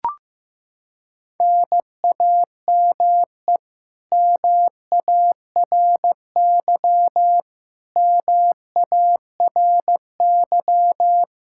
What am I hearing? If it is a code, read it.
Morse code, heard as a keyed tone, about 16 wpm, 700 Hz. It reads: NAME MARY MARY
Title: K1USN SST QSO Practice 15wpm